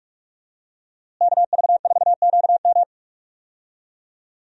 Morse code, heard as a keyed tone, about 46 wpm, 700 Hz. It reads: KV4QM
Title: USA Call Signs - 5 Characters - Repeated 2x - 45wpm